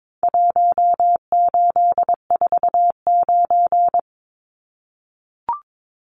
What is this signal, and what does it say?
Morse code, heard as a keyed tone, about 22 wpm, 700 Hz. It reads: 1849
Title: Numbers - Contest Style - 4-Digit 22wpm